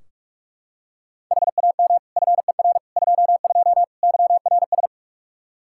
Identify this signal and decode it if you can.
Morse code, heard as a keyed tone, about 46 wpm, 700 Hz. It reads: HAM FER 22 YRS
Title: Practice QSO - Exchange 3 - More Info - 45wpm